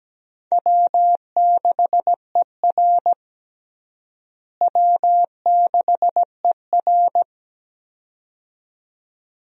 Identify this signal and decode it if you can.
Morse code, heard as a keyed tone, about 17 wpm, 700 Hz. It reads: W6ER W6ER
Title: USA Call Signs - 4 Characters - Repeated 2x - 17wpm